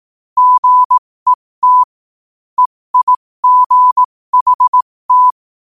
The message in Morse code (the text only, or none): GET EIGHT